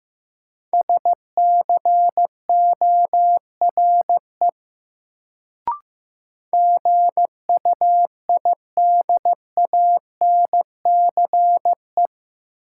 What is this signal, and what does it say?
Morse code, heard as a keyed tone, about 15 wpm, 700 Hz. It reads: SCORE GUIDANCE
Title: Sentences from Top 1600 Words - New Word Drill - 15wpm